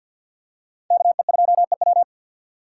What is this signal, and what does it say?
Morse code, heard as a keyed tone, about 53 wpm, 700 Hz. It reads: KE2EW
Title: USA Call Signs - 2x Character Spacing - 50wpm